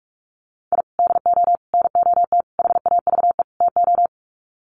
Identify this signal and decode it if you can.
Morse code, heard as a keyed tone, about 44 wpm, 700 Hz. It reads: I DO NOT HAVE TO
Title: Sets of 5 Words 45wpm